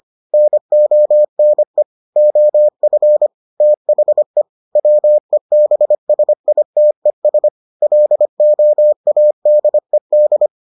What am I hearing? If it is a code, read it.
Morse code, heard as a keyed tone, about 25 wpm, 600 Hz. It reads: NONE OF THE WEBSITES LOADED